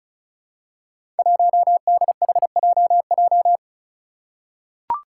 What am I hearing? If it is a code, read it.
Morse code, heard as a keyed tone, about 36 wpm, 700 Hz. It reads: 1DHJJ
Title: Five Challenging Letter-Number Combinations 35wpm